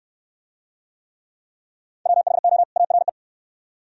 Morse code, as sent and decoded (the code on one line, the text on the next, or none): .-- .... --- / .- .-. .
WHO ARE